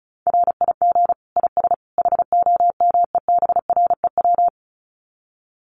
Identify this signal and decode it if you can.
Morse code, heard as a keyed tone, about 34 wpm, 700 Hz. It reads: RIG IS HOMEBREW
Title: Common QSO Phrases 35wpm